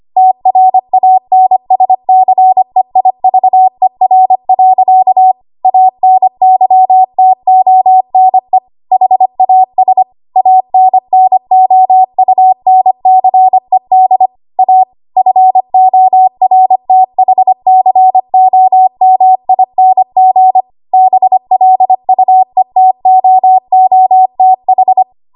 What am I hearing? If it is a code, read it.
Morse code, heard as a keyed tone, about 25 wpm, 750 Hz. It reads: TRANSCEIVER. ANYTONE HAS ANNOUNCED A FORTHCOMING BLUETOOTH
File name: Archive / 25 WPM / 25WPM